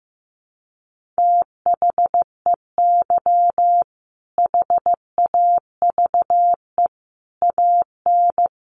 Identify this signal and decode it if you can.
Morse code, heard as a keyed tone, about 15 wpm, 700 Hz. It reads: THEY HAVE AN
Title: Sentences from Top 1500 Words 15wpm